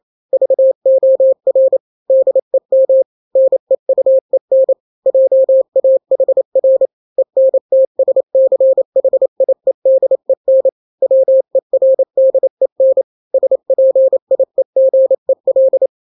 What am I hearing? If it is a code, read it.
Morse code, heard as a keyed tone, about 28 wpm, 530 Hz. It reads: VOR DEM NEUEN JAHR ENTSCHIEDEN WERDEN SPIEGEL
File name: daily / 27wpm